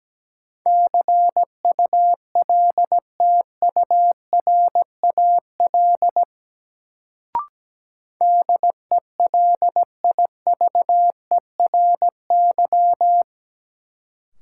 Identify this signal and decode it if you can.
Morse code, heard as a keyed tone, about 17 wpm, 700 Hz. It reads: CULTURAL DELIVERY
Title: Sentences from Top 1500 Words - New Word Drill Alt - 17wpm